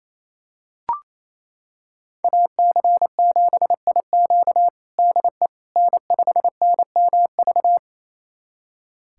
Morse code, heard as a keyed tone, about 28 wpm, 700 Hz. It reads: AC7IQ DE N5NMV
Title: USA Callsign DE Callsign - 5 Characters - 28wpm